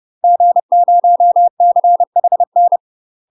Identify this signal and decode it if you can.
Morse code, heard as a keyed tone, about 30 wpm, 700 Hz. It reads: G0CHN
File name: UK MASTER QSOs TRAINING / 30WPM / UK MASTER QSOs 0015